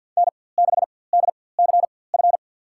Morse code, heard as a keyed tone, about 49 wpm, spaced as for 23 wpm, 700 Hz. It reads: N6DCF
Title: USA Call Signs - 5 Characters - 4x Character Spacing - 50wpm